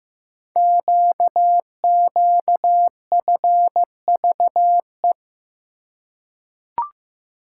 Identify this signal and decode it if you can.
Morse code, heard as a keyed tone, about 15 wpm, 700 Hz. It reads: QQFVE
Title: Code Groups - 5 Letter:Numbers - 15wpm